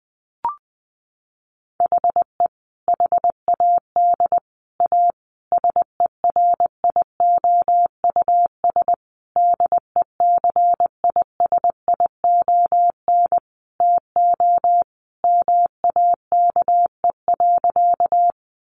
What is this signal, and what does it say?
Morse code, heard as a keyed tone, about 20 wpm, 700 Hz. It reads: HE HAD A SERIOUS DECISION TO MAKE.
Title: Sentences from Top 1100 Words 20wpm